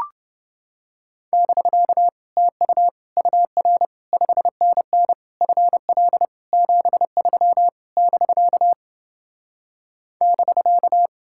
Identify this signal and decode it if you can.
Morse code, heard as a keyed tone, about 30 wpm, 700 Hz. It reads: <BK> TU UR 5NN FL 73 <BK> <BK>